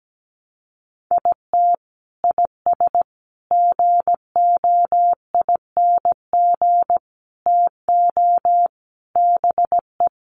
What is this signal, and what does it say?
Morse code, heard as a keyed tone, about 17 wpm, 700 Hz. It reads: IT IS GOING TO BE
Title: Sets of 5 Words 17wpm